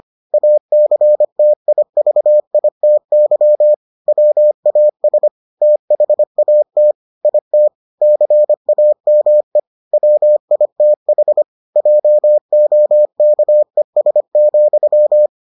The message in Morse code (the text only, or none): ACTIVITY WAS THAT IT CAME WITH JOKES,